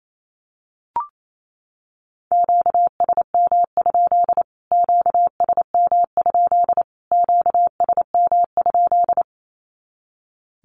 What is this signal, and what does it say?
Morse code, heard as a keyed tone, about 28 wpm, 700 Hz. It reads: QSM? QSM? QSM?
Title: Q-Codes Repeated 3x 28wpm